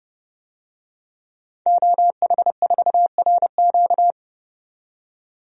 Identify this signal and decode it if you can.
Morse code, heard as a keyed tone, about 30 wpm, 700 Hz. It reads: OH4RQ
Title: DXCC-100 Call Signs - 5 Characters - 30wpm